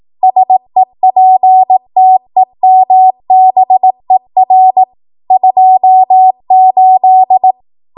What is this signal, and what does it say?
Morse code, heard as a keyed tone, about 18 wpm, 750 Hz. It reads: SEPTEMBER 28
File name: Archive / 18 WPM / 18WPM